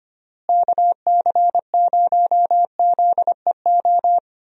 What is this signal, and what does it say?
Morse code, heard as a keyed tone, about 25 wpm, 700 Hz. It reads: KC0ZEO
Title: USA Call Signs - 6 Characters - Repeated 2x - 25wpm